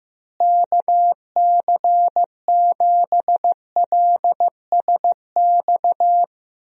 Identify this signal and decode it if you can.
Morse code, heard as a keyed tone, about 15 wpm, 700 Hz. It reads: KC7LSX
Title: USA Call Signs - 6 Characters - Repeated 3x - 15wpm